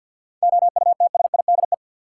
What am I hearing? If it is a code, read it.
Morse code, heard as a keyed tone, about 48 wpm, 700 Hz. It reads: OUTSIDE
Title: Sentences from Top 900 Words - New Word Drill - 50wpm